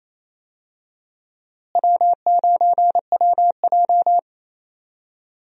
Morse code, heard as a keyed tone, about 28 wpm, 700 Hz. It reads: W9WJ
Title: USA Call Signs - 4 Characters - Repeated 3x - 28wpm